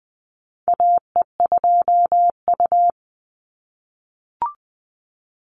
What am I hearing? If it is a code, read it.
Morse code, heard as a keyed tone, about 20 wpm, 700 Hz. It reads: AE2U
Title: Call Signs - 20wpm